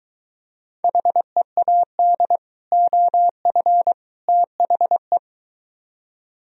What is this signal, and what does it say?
Morse code, heard as a keyed tone, about 23 wpm, 700 Hz. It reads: HEAD OF THE